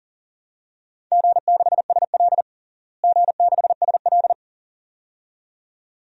Morse code, heard as a keyed tone, about 40 wpm, 700 Hz. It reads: G6SL G6SL
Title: DXCC-100 Call Signs - 4 Characters - Repeated 2x - 40wpm